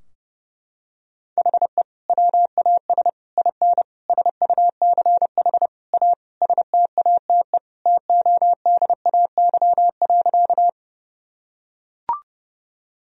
Morse code, heard as a keyed tone, about 30 wpm, 700 Hz. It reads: HE WAS IN SUCH A STATE TODAY.
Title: Sentences from Top 1000 Words - Review Part 2 - 30wpm